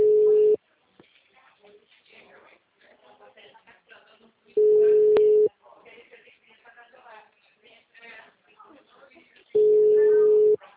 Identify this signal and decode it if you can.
Ringback tone (Europe, 425 Hz), with speech.